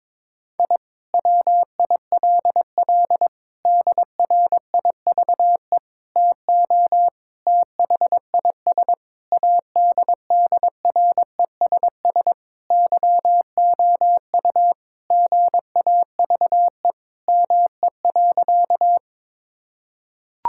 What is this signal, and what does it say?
Morse code, heard as a keyed tone, about 22 wpm, 700 Hz. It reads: I WILL DRIVE TO THIS ADDRESS YOU GAVE ME.
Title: Sentences from Top 1000 Words - Review Part 6 - 22wpm